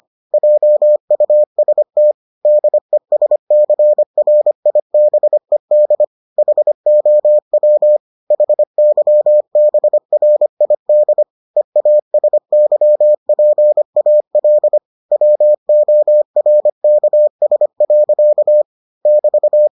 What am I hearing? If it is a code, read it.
Morse code, heard as a keyed tone, about 25 wpm, 600 Hz. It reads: JUST DESCRIBED HOW HYBRID EASYPAL WORKS. <BT>